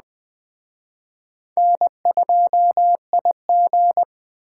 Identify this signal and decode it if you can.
Morse code, heard as a keyed tone, about 20 wpm, 700 Hz. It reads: N2IG
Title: Speed Racing - USA 1x2 Call Signs 17wpm